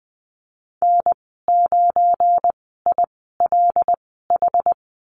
Morse code, heard as a keyed tone, about 20 wpm, spaced as for 15 wpm, 700 Hz. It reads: N9ILH
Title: 186 Lesson 30g - Letter b - Call Signs 20wpm